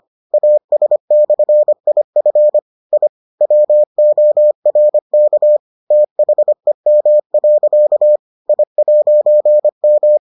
Morse code, heard as a keyed tone, about 25 wpm, 600 Hz. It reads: AS/IF I WORK THEM. I'M